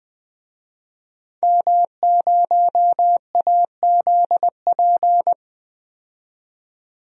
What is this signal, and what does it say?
Morse code, heard as a keyed tone, about 20 wpm, 700 Hz. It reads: M0AZP
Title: DXCC-100 Call Signs - 5 Characters - 20wpm